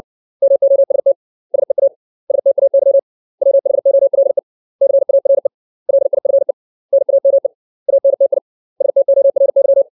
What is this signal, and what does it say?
Morse code, heard as a keyed tone, about 60 wpm, 550 Hz. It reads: MOST SEA STAY WHOLE CARE LIFE NAME ANTI STORY
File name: 632 words / 632 Words 60WPM